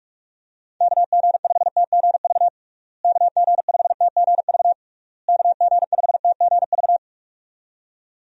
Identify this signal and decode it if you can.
Morse code, heard as a keyed tone, about 48 wpm, 700 Hz. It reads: KG5TGV KG5TGV KG5TGV
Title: USA Call Signs - 6 Characters - Repeated 3x - 45wpm